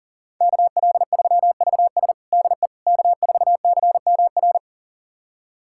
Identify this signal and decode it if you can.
Morse code, heard as a keyed tone, about 40 wpm, 700 Hz. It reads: KL3VS DE K4CMR
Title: USA Callsign DE Callsign - 5 Characters - 40wpm